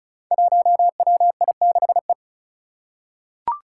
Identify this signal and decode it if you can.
Morse code, heard as a keyed tone, about 36 wpm, 700 Hz. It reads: 1WI6E
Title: Five Challenging Letter-Number Combinations 35wpm